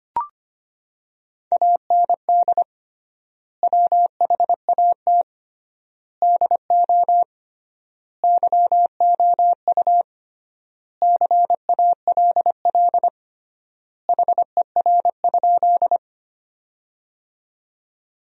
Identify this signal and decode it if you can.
Morse code, heard as a keyed tone, about 25 wpm, 700 Hz. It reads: AND WHAT DO YOU CALL HER?
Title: 3x Word-Spacing - Sentences from Top 100 Words - 25wpm